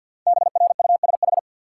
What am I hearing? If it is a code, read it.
Morse code, heard as a keyed tone, about 50 wpm, 700 Hz. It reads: BRUSH
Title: Sentences from Top 1100 Words - New Word Drill - 50wpm